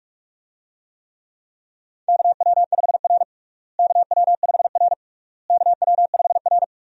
Morse code, heard as a keyed tone, about 46 wpm, 700 Hz. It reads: KW5R KW5R KW5R
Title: USA Call Signs - 4 Characters - Repeated 3x - 45wpm